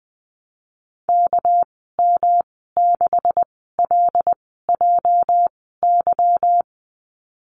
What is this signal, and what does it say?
Morse code, heard as a keyed tone, about 20 wpm, spaced as for 15 wpm, 700 Hz. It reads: KM6LJY
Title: 207 Lesson 33g - Letter j - Call Signs 20wpm